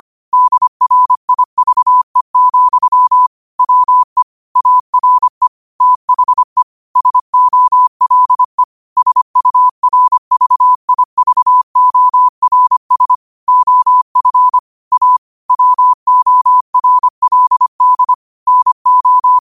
Morse code, heard as a keyed tone, about 25 wpm, 1000 Hz. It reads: DRIVE, WE ARE THE SOLE SURVIVORS OF A WORLD NO